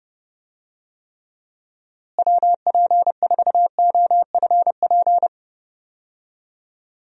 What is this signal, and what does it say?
Morse code, heard as a keyed tone, about 30 wpm, 700 Hz. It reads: WP4OFP
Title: USA Call Signs - 6 Characters - Repeated 2x - 30wpm